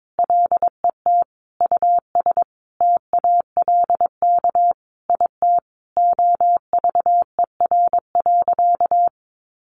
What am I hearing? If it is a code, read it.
Morse code, heard as a keyed tone, about 22 wpm, 700 Hz. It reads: LET US TALK IT OVER.
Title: Sentences from Top 500 Words 22wpm